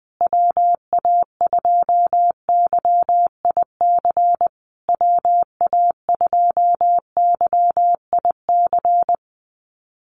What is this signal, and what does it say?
Morse code, heard as a keyed tone, about 20 wpm, 700 Hz. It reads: WA2YIC WA2YIC
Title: USA Call Signs Repeated 3x 20wpm